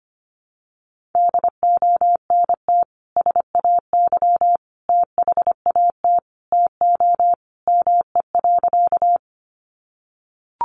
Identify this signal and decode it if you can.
Morse code, heard as a keyed tone, about 25 wpm, 700 Hz. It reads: DONT SAY THAT TO ME.
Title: Sentences from Top 300 Words 25wpm